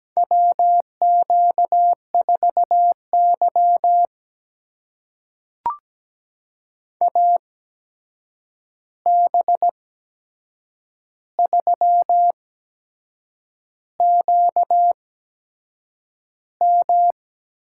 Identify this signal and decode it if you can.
Morse code, heard as a keyed tone, about 17 wpm, spaced as for 7 wpm, 700 Hz. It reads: WQ4Y AB3QM
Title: USA Call Signs - 8x Character Spacing - 17wpm